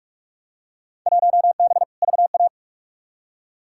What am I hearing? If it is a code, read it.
Morse code, heard as a keyed tone, about 46 wpm, 700 Hz. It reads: 1B VA
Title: Field Day Exchanges - 45wpm